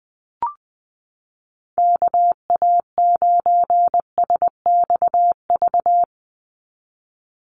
Morse code, heard as a keyed tone, about 20 wpm, 700 Hz. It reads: KA9SXV
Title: Call Signs - 20wpm